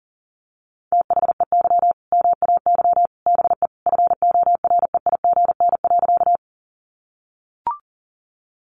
Morse code, heard as a keyed tone, about 40 wpm, 700 Hz. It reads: THEY MAY BE FOREIGN.
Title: Sentences from Top 1400 Words 40wpm